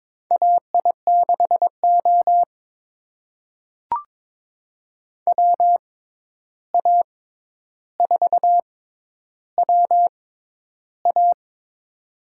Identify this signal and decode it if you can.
Morse code, heard as a keyed tone, about 22 wpm, spaced as for 11 wpm, 700 Hz. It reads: AI6O WA4WA
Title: USA Call Signs - 6x Character Spacing - 22wpm